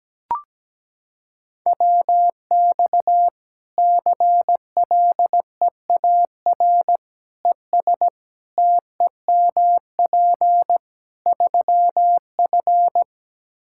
Morse code, heard as a keyed tone, about 17 wpm, 700 Hz. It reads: WX CLEAR ES TEMP 3F
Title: Practice QSO - Exchange 2 - Info - 17wpm